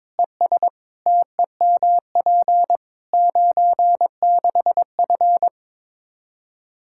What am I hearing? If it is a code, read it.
Morse code, heard as a keyed tone, about 22 wpm, 700 Hz. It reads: ES TEMP 96F
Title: Practice QSO - Exchange 2 - Info - 22wpm